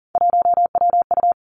Morse code, heard as a keyed tone, about 41 wpm, 700 Hz. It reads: 1WU
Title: Three Challenging Letter-Number Combinations 40wpm